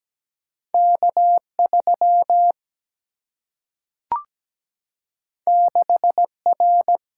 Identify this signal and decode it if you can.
Morse code, heard as a keyed tone, about 17 wpm, 700 Hz. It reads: K3 6R
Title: Single-Letter Single-Number 17wpm